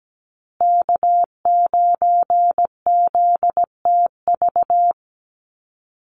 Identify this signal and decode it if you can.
Morse code, heard as a keyed tone, about 17 wpm, 700 Hz. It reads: K9ZTV
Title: Call Signs Repeated 2x - 17wpm